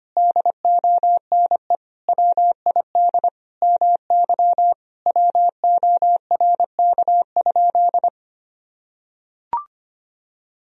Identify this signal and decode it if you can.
Morse code, heard as a keyed tone, about 25 wpm, 700 Hz. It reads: DONE WID MY WORK?
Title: Sentences from Top 500 Words and Abbreviations 25wpm